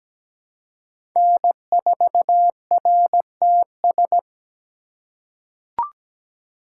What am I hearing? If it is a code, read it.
Morse code, heard as a keyed tone, about 17 wpm, 700 Hz. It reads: N4RTS
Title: USA Call Signs - 3x Character Spacing - 17wpm